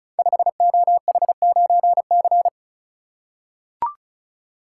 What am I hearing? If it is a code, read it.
Morse code, heard as a keyed tone, about 36 wpm, 700 Hz. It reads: 5OH9C
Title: Code Groups - 5 Letter:Numbers - 35wpm